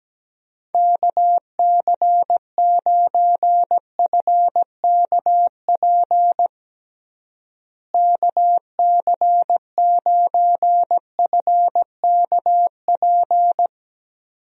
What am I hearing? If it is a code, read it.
Morse code, heard as a keyed tone, about 17 wpm, 700 Hz. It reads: KC9FKP KC9FKP
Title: USA Call Signs - 6 Characters - Repeated 2x - 17wpm